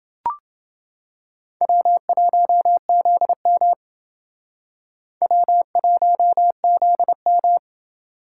Morse code, heard as a keyed tone, about 29 wpm, 700 Hz. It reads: W1ZM W1ZM